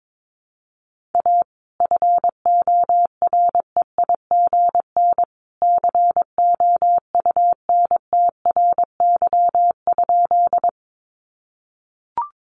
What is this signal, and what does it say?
Morse code, heard as a keyed tone, about 22 wpm, 700 Hz. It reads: A FOREIGN COUNTRY?
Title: Sentences from Top 1400 Words 22wpm